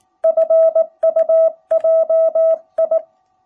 Morse code, heard as a keyed tone, about 19 wpm, 660 Hz. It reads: FUJI
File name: jh8cbh / 2021cw